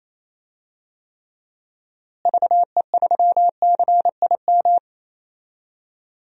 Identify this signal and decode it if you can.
Morse code, heard as a keyed tone, about 28 wpm, 700 Hz. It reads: VE3CIM
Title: DXCC-100 Call Signs - 6 Characters - Repeated 2x - 28wpm